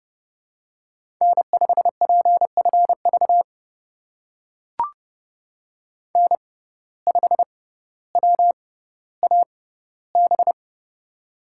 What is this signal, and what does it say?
Morse code, heard as a keyed tone, about 29 wpm, spaced as for 15 wpm, 700 Hz. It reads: N5PFV N5WAB
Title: USA Call Signs - 5 Characters - 6x Character Spacing - 30wpm